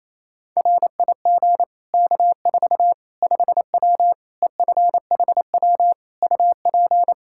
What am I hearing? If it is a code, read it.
Morse code, heard as a keyed tone, about 28 wpm, 700 Hz. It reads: RIG K4 5W EFHW UP
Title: Practice QSO - Exchange 2 - Info - 28wpm